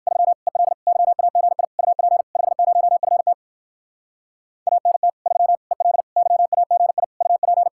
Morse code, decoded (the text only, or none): 3 EL YAGI UP 50FT ANT 3 EL YAGI UP